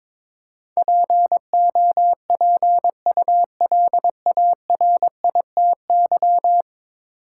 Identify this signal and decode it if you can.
Morse code, heard as a keyed tone, about 22 wpm, 700 Hz. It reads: POPULARITY
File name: practice / 22wpm / 10 Letter Words - First 3 Letters Given - 22wpm